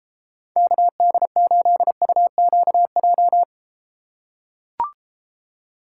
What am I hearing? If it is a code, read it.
Morse code, heard as a keyed tone, about 33 wpm, 700 Hz. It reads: KD8UQJ